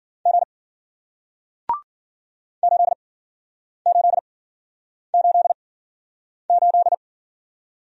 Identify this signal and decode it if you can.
Morse code, heard as a keyed tone, about 52 wpm, spaced as for 9 wpm, 700 Hz. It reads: G 8888